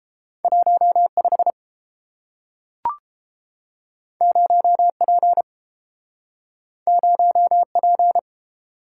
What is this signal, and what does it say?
Morse code, heard as a keyed tone, about 30 wpm, 700 Hz. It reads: 15 0P 0P